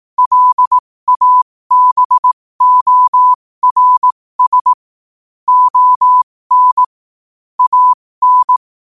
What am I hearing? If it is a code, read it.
Morse code, heard as a keyed tone, about 18 wpm, spaced as for 15 wpm, 1000 Hz. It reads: LABORS ON AN
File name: qotdfiles / 15wpm